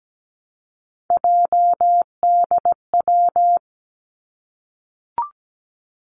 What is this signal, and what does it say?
Morse code, heard as a keyed tone, about 17 wpm, 700 Hz. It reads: JDW
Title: Three Challenging Letter-Number Combinations 17wpm